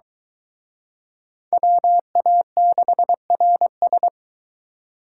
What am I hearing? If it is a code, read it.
Morse code, heard as a keyed tone, about 23 wpm, 700 Hz. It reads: WA6RS